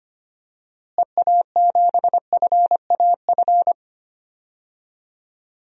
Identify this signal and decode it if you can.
Morse code, heard as a keyed tone, about 25 wpm, 700 Hz. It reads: EA7FAF
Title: DXCC-100 Call Signs - 6 Characters - Repeated 3x - 25wpm